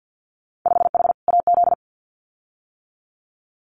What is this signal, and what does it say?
Morse code, heard as a keyed tone, about 50 wpm, 700 Hz. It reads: 5H AZ
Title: Winter Field Day Exchanges Encore 50wpm